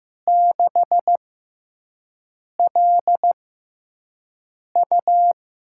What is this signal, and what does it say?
Morse code, heard as a keyed tone, about 15 wpm, spaced as for 5 wpm, 700 Hz. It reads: 6LU
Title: USA Call Signs - 5 Characters - 6x Character Spacing - 15wpm